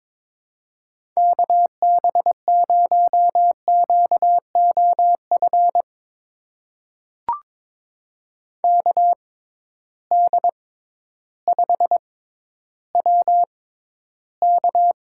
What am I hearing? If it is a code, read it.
Morse code, heard as a keyed tone, about 22 wpm, spaced as for 12 wpm, 700 Hz. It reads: KB0QOF KD5WK
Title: USA Call Signs - 6 Characters - 6x Character Spacing - 22wpm